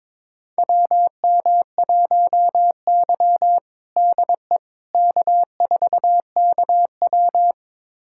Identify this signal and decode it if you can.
Morse code, heard as a keyed tone, about 22 wpm, 700 Hz. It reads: WM1Y DE K4KW
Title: USA Callsign DE Callsign - 4 Characters - 22wpm